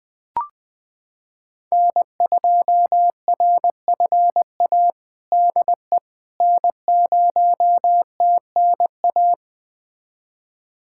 Morse code, heard as a keyed tone, about 20 wpm, 700 Hz. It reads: N2RFA DE N0TNA
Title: USA Callsign DE Callsign - 5 Characters - 20wpm